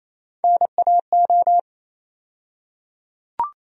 Morse code, heard as a keyed tone, about 28 wpm, 700 Hz. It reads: NAO
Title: USA Call Sign Postfixes - 3 Letters 28wpm